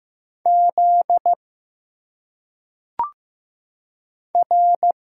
Morse code, heard as a keyed tone, about 15 wpm, 700 Hz. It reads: Z R
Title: Single Letters - Phonetic Pronunciation 15wpm